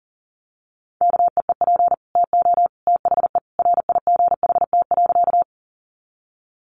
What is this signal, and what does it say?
Morse code, heard as a keyed tone, about 40 wpm, 700 Hz. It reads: KEEP TO THE RIGHT.
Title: Sentences from Top 300 Words 40wpm